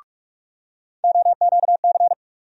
Morse code, heard as a keyed tone, about 47 wpm, 700 Hz. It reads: OQC
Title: USA Call Sign Postfixes - 3 Letters 45wpm